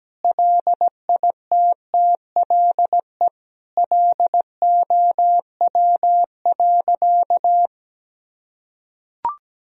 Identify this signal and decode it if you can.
Morse code, heard as a keyed tone, about 17 wpm, 700 Hz. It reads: LITTLE LOW.
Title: Sentences from Top 1000 Words - Review Part 1 - 17wpm